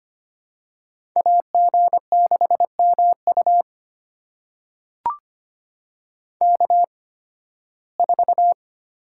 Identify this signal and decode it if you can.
Morse code, heard as a keyed tone, about 25 wpm, spaced as for 16 wpm, 700 Hz. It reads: AG6MU K4
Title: USA Call Signs - 5 Characters - 8x Character Spacing - 25wpm